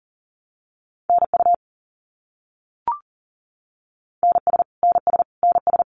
Morse code, heard as a keyed tone, about 40 wpm, 700 Hz. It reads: NU NS NS NS
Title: Canadian Provinces and Territories - Repeated 3x - 40wpm